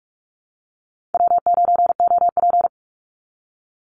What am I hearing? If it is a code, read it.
Morse code, heard as a keyed tone, about 44 wpm, 700 Hz. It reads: W9OP
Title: Call Signs Repeated 2x - 45wpm